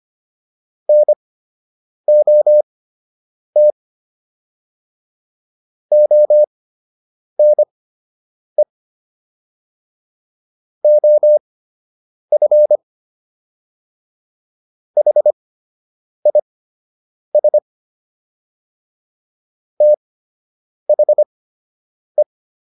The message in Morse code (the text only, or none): NOT ONE OF HIS THE